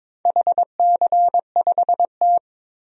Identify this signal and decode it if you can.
Morse code, heard as a keyed tone, about 22 wpm, 700 Hz. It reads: HC5T
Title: DXCC-100 Call Signs - 4 Characters - 22wpm